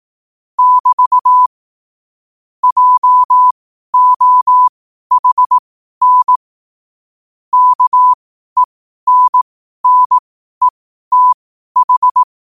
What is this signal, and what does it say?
Morse code, heard as a keyed tone, about 18 wpm, spaced as for 12 wpm, 1000 Hz. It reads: <BT> JOHN KENNETH
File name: qotdfiles / 12wpm